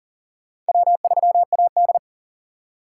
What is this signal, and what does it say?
Morse code, heard as a keyed tone, about 40 wpm, 700 Hz. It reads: W3AD
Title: USA Call Signs - 8x Character Spacing - 40wpm